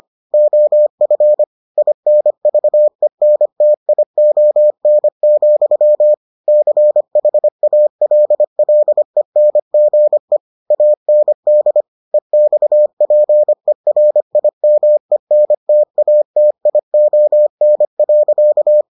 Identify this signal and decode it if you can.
Morse code, heard as a keyed tone, about 25 wpm, 600 Hz. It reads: OF INVENTION, CHALLENGE AND EXPERIMENTATION.